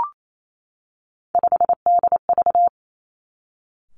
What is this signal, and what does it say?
Morse code, heard as a keyed tone, about 28 wpm, 700 Hz. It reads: 5DV
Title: Three Challenging Letter-Number Combinations 28wpm